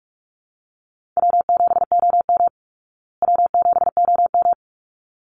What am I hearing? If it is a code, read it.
Morse code, heard as a keyed tone, about 44 wpm, 700 Hz. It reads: W7OM W7OM
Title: Easy Call Signs Repeated 2x - 45wpm